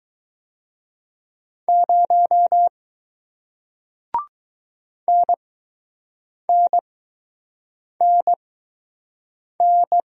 Morse code, heard as a keyed tone, about 20 wpm, spaced as for 6 wpm, 700 Hz. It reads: 0 NNNN